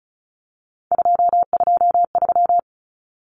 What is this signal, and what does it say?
Morse code, heard as a keyed tone, about 34 wpm, 700 Hz. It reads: 223
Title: Common Cut Numbers - 3-Digit 35wpm